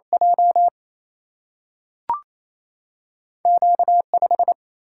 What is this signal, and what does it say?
Morse code, heard as a keyed tone, about 28 wpm, 700 Hz. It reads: J Q 5
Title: Single-Letter Single-Number 28wpm